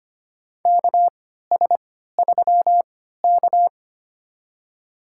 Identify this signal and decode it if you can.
Morse code, heard as a keyed tone, about 25 wpm, spaced as for 14 wpm, 700 Hz. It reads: KS3K
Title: USA Call Signs - 4 Characters - 3x Character Spacing - 25wpm